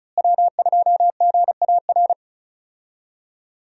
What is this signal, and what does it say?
Morse code, heard as a keyed tone, about 34 wpm, 700 Hz. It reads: W2GAR
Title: USA Call Signs - 5 Characters - Repeated 2x - 35wpm